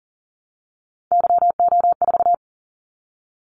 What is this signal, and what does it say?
Morse code, heard as a keyed tone, about 38 wpm, 700 Hz. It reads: YO4
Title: Callsign Prefixes for Top 25 DXCC - 40wpm